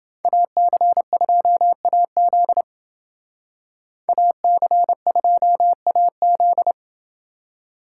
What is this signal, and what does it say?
Morse code, heard as a keyed tone, about 29 wpm, 700 Hz. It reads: AC2AZ AC2AZ